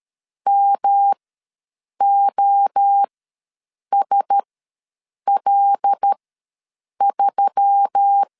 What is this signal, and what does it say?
Morse code, heard as a keyed tone, about 13 wpm, spaced as for 7 wpm, 800 Hz. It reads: MOSL3
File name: LEZIONI / 5 LBVC / 5 L ws 12wpm 2958